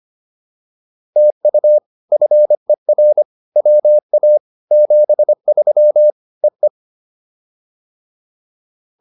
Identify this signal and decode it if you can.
Morse code, heard as a keyed tone, about 25 wpm, 600 Hz. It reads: TU FER WA 73 EE
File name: POTA301a 25